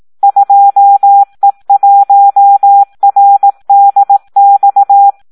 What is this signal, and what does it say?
Morse code, heard as a keyed tone, about 18 wpm, 800 Hz. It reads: 2E1RDX
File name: Audio practica CW / SD 1 18